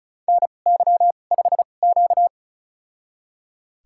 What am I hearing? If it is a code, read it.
Morse code, heard as a keyed tone, about 36 wpm, spaced as for 26 wpm, 700 Hz. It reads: NY5Q
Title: USA Call Signs - 4 Characters - 2x Character Spacing - 35wpm